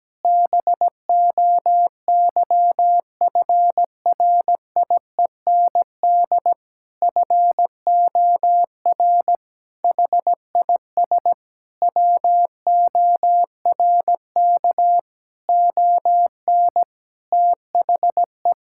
Morse code, heard as a keyed tone, about 17 wpm, 700 Hz. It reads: BOYFRIEND FOR HIS WORK ON THE
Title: Sentences from Top 1000 Words - Review Part 5 - 17wpm